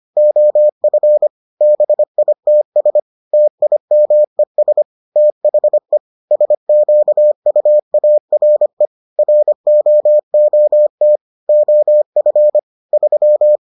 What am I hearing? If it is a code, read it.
Morse code, heard as a keyed tone, about 25 wpm, 600 Hz. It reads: OF BITS TIMES THE SQUARE ROOT OF 3